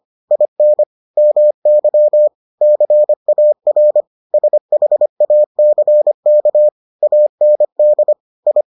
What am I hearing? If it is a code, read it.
Morse code, heard as a keyed tone, about 25 wpm, 600 Hz. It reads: IN MY CAR SHACK AND I